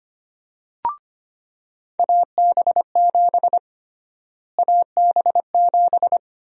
Morse code, heard as a keyed tone, about 25 wpm, 700 Hz. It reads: AB7 AB7
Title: USA Call Sign Prefixes - Repeated 2x - 25wpm